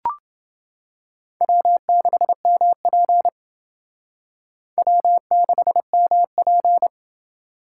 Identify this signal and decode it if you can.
Morse code, heard as a keyed tone, about 28 wpm, 700 Hz. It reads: W6MP W6MP